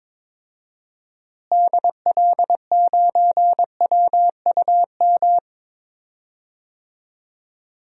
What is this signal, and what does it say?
Morse code, heard as a keyed tone, about 22 wpm, 700 Hz. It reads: DL9WUM
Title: DXCC-100 Call Signs - 6 Characters - Repeated 2x - 22wpm